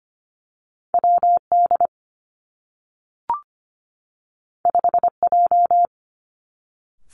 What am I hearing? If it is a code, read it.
Morse code, heard as a keyed tone, about 25 wpm, 700 Hz. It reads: WD 5J
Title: Two Challenging Letter-Number Combinations 25wpm